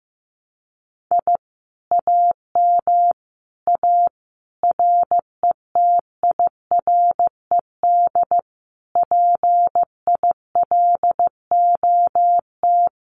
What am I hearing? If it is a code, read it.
Morse code, heard as a keyed tone, about 15 wpm, 700 Hz. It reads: I AM A RETIRED PILOT
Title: Common QSO Phrases Encore 15wpm